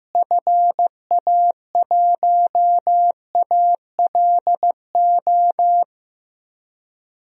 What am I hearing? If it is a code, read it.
Morse code, heard as a keyed tone, about 16 wpm, 700 Hz. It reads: FA1ALO
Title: DXCC-100 Call Signs - 6 Characters - Repeated 2x - 15wpm